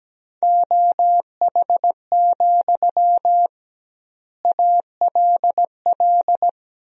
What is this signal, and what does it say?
Morse code, heard as a keyed tone, about 17 wpm, 700 Hz. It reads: OH, ALL
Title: 2x Word-Spacing - Sentences from Top 500 Words - 17wpm